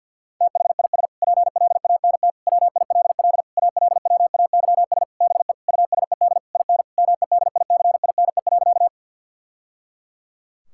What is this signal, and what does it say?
Morse code, heard as a keyed tone, about 50 wpm, 700 Hz. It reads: THIS PLANT WILL ALWAYS BE USED IN MEDICINE.
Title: Sentences from Top 1000 Words - Review Part 6 - 50wpm